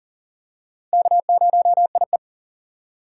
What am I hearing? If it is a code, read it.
Morse code, heard as a keyed tone, about 40 wpm, 700 Hz. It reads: K0IE
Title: USA Call Signs - 4 Characters - 2x Character Spacing - 40wpm